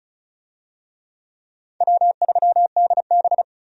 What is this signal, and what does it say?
Morse code, heard as a keyed tone, about 36 wpm, 700 Hz. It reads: W3DB
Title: USA Call Signs - 4 Characters - 35wpm